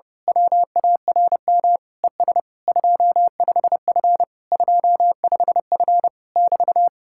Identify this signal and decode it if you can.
Morse code, heard as a keyed tone, about 28 wpm, 700 Hz. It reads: WARM ES 25F 25F <BT>
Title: Speed-Racing Common QSO Phrases - 25wpm